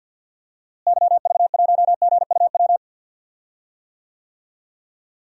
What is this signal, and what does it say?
Morse code, heard as a keyed tone, about 51 wpm, 700 Hz. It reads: YV1GUW
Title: DXCC-100 Call Signs - 6 Characters - Repeated 3x - 50wpm